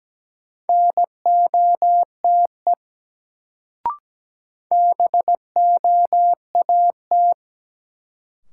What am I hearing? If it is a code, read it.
Morse code, heard as a keyed tone, about 17 wpm, 700 Hz. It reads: NOTE BOAT
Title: Sentences from Top 500 Words - New Word Drill - 17wpm